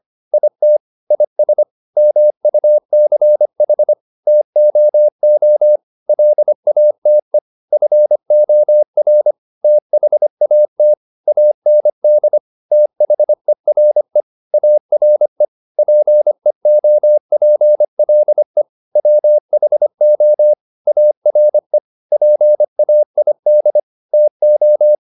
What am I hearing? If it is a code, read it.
Morse code, heard as a keyed tone, about 25 wpm, 600 Hz. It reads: IT IS MUCH TOO LATE FOR THAT AND THERE ARE PEOPLE WHO ARE PAID TO